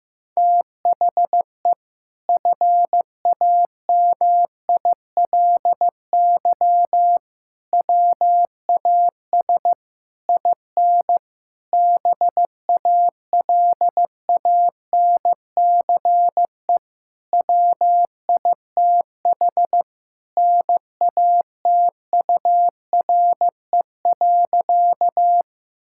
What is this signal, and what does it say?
Morse code, heard as a keyed tone, about 16 wpm, 700 Hz. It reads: THE FAMILY WAS IN BALANCE WITH NATURE.
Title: Sentences from Top 1000 Words - Review Part 6 - 15wpm